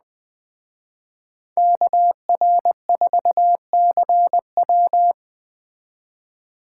Morse code, heard as a keyed tone, about 20 wpm, 700 Hz. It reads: KR4CW
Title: Speed Racing - USA Call Signs 17wpm